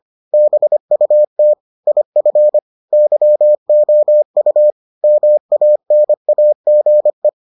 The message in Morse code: BUT IF YOU MANAGE

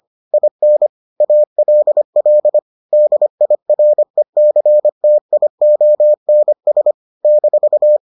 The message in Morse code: IN ALL DIRECTIONS -

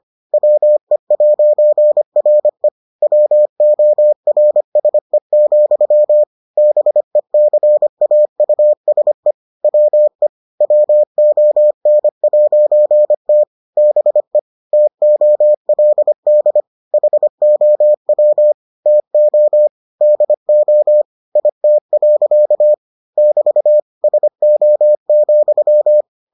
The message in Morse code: WE'RE WORSE, BECAUSE WE WON'T BE TOLD HOW TO DO IT. <BT> SO,

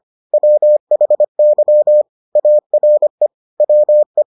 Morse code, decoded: WHY ARE WE

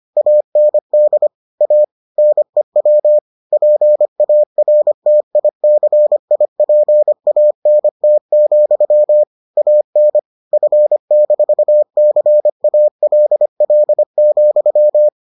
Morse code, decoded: AND A NEW PARTICIPANT, AN F-CALL,

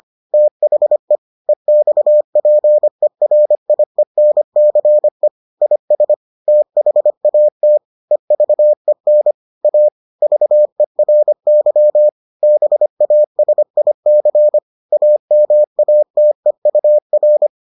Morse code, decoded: THE EXPERIENCE IS THAT EVEN A VERY BASIC AMATEUR

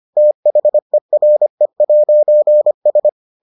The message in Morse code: THERE'S

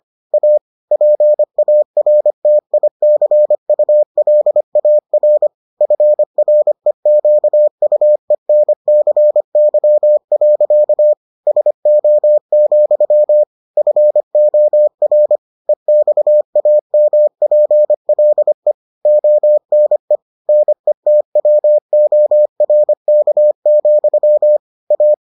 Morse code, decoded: A PARTICULAR FREQUENCY. SO, FOR EXAMPLE ONE NETWORK, A

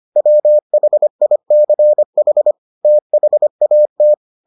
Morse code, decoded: WHICH THAT

